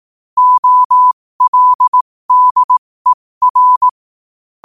Morse code, heard as a keyed tone, about 18 wpm, spaced as for 15 wpm, 1000 Hz. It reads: OLDER
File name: qotdfiles / 15wpm